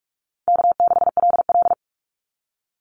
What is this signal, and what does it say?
Morse code, heard as a keyed tone, about 46 wpm, 700 Hz. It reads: K6LL
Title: Easy Call Signs - 45wpm